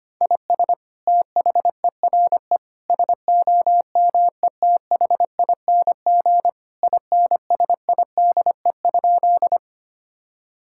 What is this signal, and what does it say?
Morse code, heard as a keyed tone, about 25 wpm, 700 Hz. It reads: IS THERE SOMETHING INSIDE?
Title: Sentences from Top 1000 Words - Review Part 6 - 25wpm